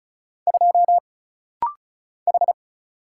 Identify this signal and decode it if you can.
Morse code, heard as a keyed tone, about 35 wpm, 700 Hz. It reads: 2 H
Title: Single Letter-Number - Rapid-Fire 35wpm